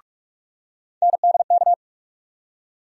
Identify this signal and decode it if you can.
Morse code, heard as a keyed tone, about 44 wpm, 700 Hz. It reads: NDK
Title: USA Call Sign Postfixes - 3 Letters 45wpm